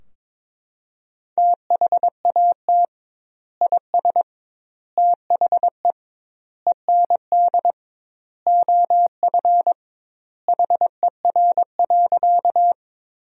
Morse code, heard as a keyed tone, about 22 wpm, 700 Hz. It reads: THAT IS THE END OF HER.
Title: 2x Word-Spacing - Sentences from Top 200 Words - 22wpm